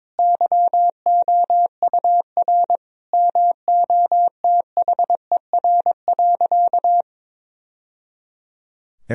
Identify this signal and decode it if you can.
Morse code, heard as a keyed tone, about 22 wpm, 700 Hz. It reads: YOUR MOTHER.
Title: Sentences from Top 1000 Words - Review Part 4 - 22wpm